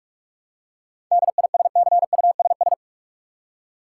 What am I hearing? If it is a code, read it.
Morse code, heard as a keyed tone, about 46 wpm, 700 Hz. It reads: DISCUSS
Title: Seven Letter Words - 45wpm